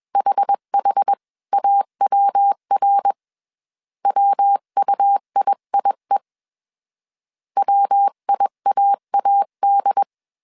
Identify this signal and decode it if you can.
Morse code, heard as a keyed tone, about 21 wpm, 800 Hz. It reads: HHAWR WUIIE WIAAD
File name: CW / DET / 1 3 1500C S B 20wpm 1754